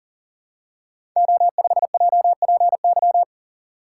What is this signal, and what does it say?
Morse code, heard as a keyed tone, about 41 wpm, 700 Hz. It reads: O5JPY
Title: Code Groups - 5 Letter:Numbers - 40wpm